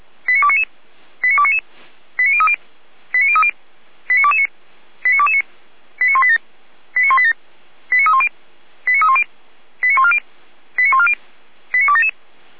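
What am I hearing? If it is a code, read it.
Five-tone selective call (ZVEI), heard as quick runs of five tones, tones of about 70 ms. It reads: 89299 89299 89030 89030 89229 89229 89187 89187 89310 89310 89140 89140 89277
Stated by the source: ZVEI five-tone selective call received by SDR, from the fire brigade (Feuerwehr) Orlamünde.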